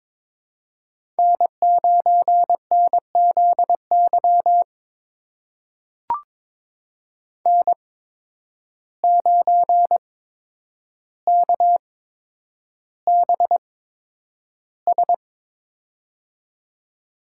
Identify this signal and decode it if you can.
Morse code, heard as a keyed tone, about 22 wpm, spaced as for 9 wpm, 700 Hz. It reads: N9NZY N9KBS
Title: USA Call Signs - 5 Characters - 8x Character Spacing - 22wpm